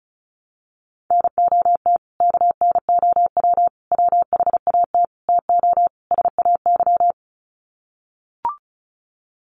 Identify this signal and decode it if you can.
Morse code, heard as a keyed tone, about 34 wpm, 700 Hz. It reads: NOT KNOW WHAT TO SAY
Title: Sets of 5 Words 35wpm